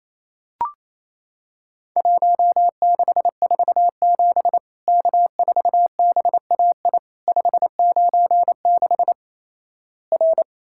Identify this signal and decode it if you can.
Morse code, heard as a keyed tone, about 28 wpm, 700 Hz. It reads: 1647 K4BAI 596 R
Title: Worked All Europe DX Contest - QTC Exchange - 28wpm